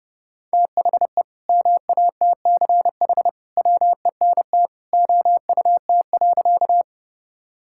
Morse code, heard as a keyed tone, about 30 wpm, 700 Hz. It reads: THE MATCH WENT OUT.
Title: Sentences from Top 1600 Words 30wpm